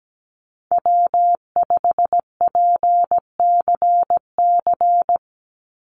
Morse code, heard as a keyed tone, about 17 wpm, 700 Hz. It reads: W5PCC
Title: USA Call Signs 17wpm